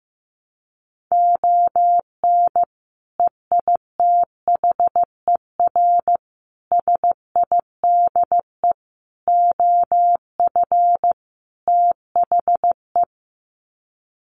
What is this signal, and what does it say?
Morse code, heard as a keyed tone, about 15 wpm, 700 Hz. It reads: ON EITHER SIDE OF THE
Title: Sets of 5 Words 15wpm